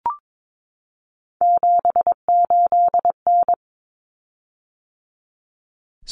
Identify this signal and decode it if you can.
Morse code, heard as a keyed tone, about 22 wpm, 700 Hz. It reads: 78N
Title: Common Cut Numbers - 3-Digit 22wpm